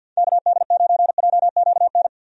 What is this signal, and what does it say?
Morse code, heard as a keyed tone, about 47 wpm, 700 Hz. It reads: KD9JQN
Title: USA Call Signs - 6 Characters - 50wpm